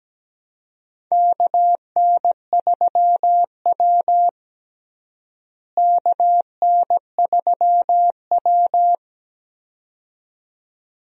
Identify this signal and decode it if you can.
Morse code, heard as a keyed tone, about 17 wpm, 700 Hz. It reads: KN3W KN3W
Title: USA Call Signs - 4 Characters - Repeated 2x - 17wpm